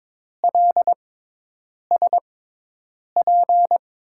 Morse code, heard as a keyed tone, about 22 wpm, spaced as for 8 wpm, 700 Hz. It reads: LSP
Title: USA Call Signs - 5 Characters - 6x Character Spacing - 22wpm